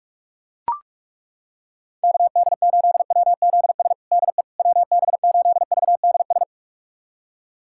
Morse code, heard as a keyed tone, about 41 wpm, 700 Hz. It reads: KD8WZS DE WB8VDS
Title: USA Callsign DE Callsign - 6 Characters - 45wpm